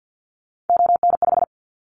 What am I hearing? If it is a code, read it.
Morse code, heard as a keyed tone, about 50 wpm, 700 Hz. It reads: ON5
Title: Callsign Prefixes for Top 100 DXCC - 50wpm